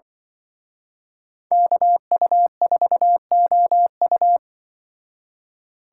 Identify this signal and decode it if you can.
Morse code, heard as a keyed tone, about 24 wpm, 700 Hz. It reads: KU4OU
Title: Speed Racing - USA 2x2 Call Signs 20wpm